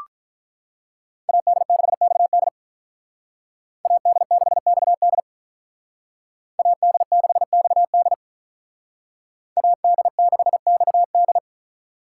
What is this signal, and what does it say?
Morse code, heard as a keyed tone, about 42 wpm, 700 Hz. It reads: AD6XD AD6XD AD6XD AD6XD
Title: Speed Racing - USA 2x2 Call Signs 35wpm